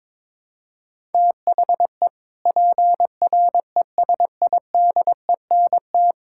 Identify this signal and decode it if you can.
Morse code, heard as a keyed tone, about 22 wpm, 700 Hz. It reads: THE PRESIDENT
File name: practice / 22wpm / Sets of 6 Words - 22wpm